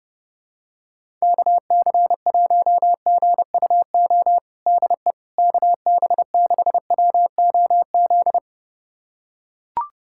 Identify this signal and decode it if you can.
Morse code, heard as a keyed tone, about 30 wpm, 700 Hz. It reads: KC1GUO DE KB6WOZ
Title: USA Callsign DE Callsign 30wpm